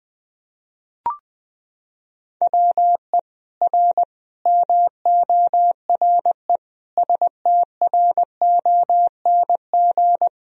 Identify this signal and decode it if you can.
Morse code, heard as a keyed tone, about 20 wpm, 700 Hz. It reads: WE R MORE STRONG
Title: Sentences from Top 500 Words and Abbreviations 20wpm